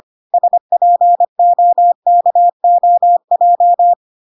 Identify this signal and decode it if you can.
Morse code, heard as a keyed tone, about 25 wpm, 700 Hz. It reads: SPOKOJ